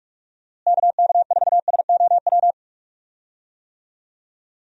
Morse code, heard as a keyed tone, about 45 wpm, 700 Hz. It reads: KK4SOW
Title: USA Call Signs - 6 Characters - Repeated 2x - 45wpm